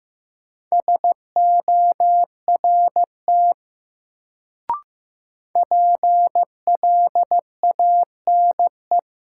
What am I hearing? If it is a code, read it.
Morse code, heard as a keyed tone, about 15 wpm, 700 Hz. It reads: SORT PLANE
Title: Sentences from Top 1000 Words - Review Part 3 - New Word Drill - 15wpm